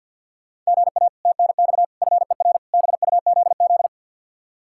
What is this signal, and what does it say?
Morse code, heard as a keyed tone, about 62 wpm, 700 Hz. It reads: GA TNX FER BUZZ
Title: Speed-Racing Common QSO Phrases - 50wpm